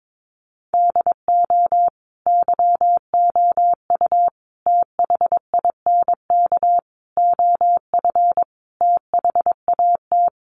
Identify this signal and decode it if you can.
Morse code, heard as a keyed tone, about 22 wpm, 700 Hz. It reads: DO YOU THINK OF THAT
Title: Sets of 5 Words Encore2 22wpm